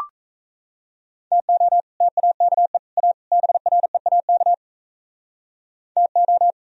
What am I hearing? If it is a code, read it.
Morse code, heard as a keyed tone, about 36 wpm, 690 Hz. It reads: TO TAKE A BREAK TO